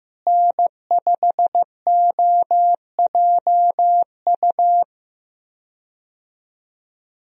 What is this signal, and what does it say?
Morse code, heard as a keyed tone, about 14 wpm, 700 Hz. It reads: N5OJU
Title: USA Call Signs - 5 Characters - Repeated 2x - 15wpm